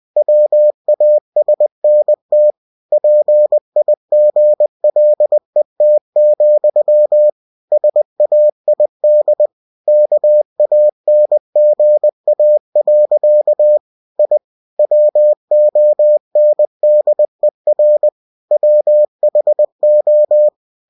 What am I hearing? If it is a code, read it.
Morse code, heard as a keyed tone, about 20 wpm, 600 Hz. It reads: WASNT PIGLET, SAID KANGA. I WONDER WHO
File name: cw-books / winnie-the-pooh / winnie-20wpm-page-0074